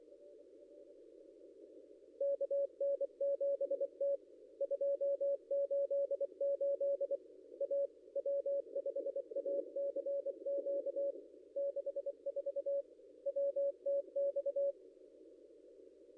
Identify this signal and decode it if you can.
Morse code, heard as a keyed tone, about 24 wpm, 560 Hz. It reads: KN7T 288 A W5ACQ 64 WTX
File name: contests / ss-cw-k5tr-op-n5rz -0800